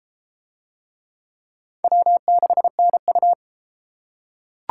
Morse code, heard as a keyed tone, about 30 wpm, 700 Hz. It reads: W6NU